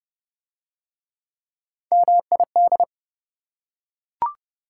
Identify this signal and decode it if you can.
Morse code, heard as a keyed tone, about 27 wpm, 700 Hz. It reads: MID